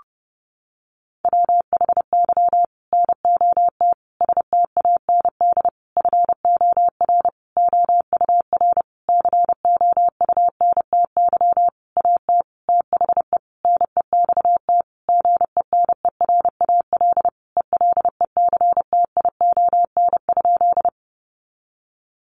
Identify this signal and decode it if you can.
Morse code, heard as a keyed tone, about 30 wpm, 700 Hz. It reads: WHY NOT STAND FOR OUR COUNTY AT THE NEXT GENERAL ELECTION?
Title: Sentences from Top 1500 Words 30wpm